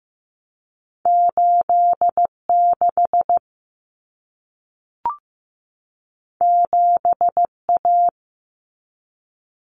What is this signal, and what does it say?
Morse code, heard as a keyed tone, about 15 wpm, 700 Hz. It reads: 86 7A
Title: Common Cut Numbers - 2-Digit 15wpm